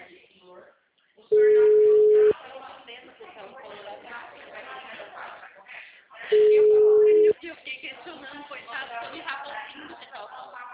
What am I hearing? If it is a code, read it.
Ringback tone (Europe, 425 Hz), with speech.
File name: out- -2003- - - .49028